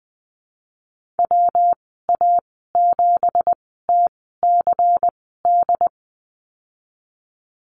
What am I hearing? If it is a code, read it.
Morse code, heard as a keyed tone, about 20 wpm, spaced as for 15 wpm, 700 Hz. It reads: WA7TCD
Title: 238 Lesson 38g - Letter q - Call Signs 20wpm